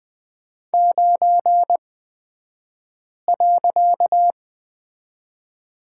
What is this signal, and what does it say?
Morse code, heard as a keyed tone, about 20 wpm, spaced as for 5 wpm, 700 Hz. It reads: 9.
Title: 255 Lesson 41e - Period - Three Characters 20wpm